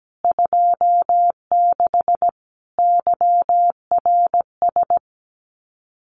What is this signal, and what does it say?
Morse code, heard as a keyed tone, about 17 wpm, 700 Hz. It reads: 26 YRS
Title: Common QSO Phrases Encore 17wpm